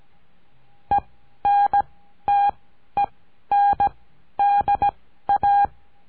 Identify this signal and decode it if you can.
Morse code, heard as a keyed tone, about 17 wpm, spaced as for 12 wpm, 810 Hz. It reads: ENTENDA
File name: CURSO DE TELEGRAFIA